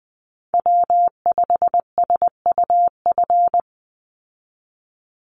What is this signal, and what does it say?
Morse code, heard as a keyed tone, about 20 wpm, 700 Hz. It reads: W5SUF
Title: USA Call Signs 20wpm